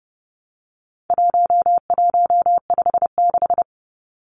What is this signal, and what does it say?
Morse code, heard as a keyed tone, about 30 wpm, 700 Hz. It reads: 1156
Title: Numbers - 4-Digit 30wpm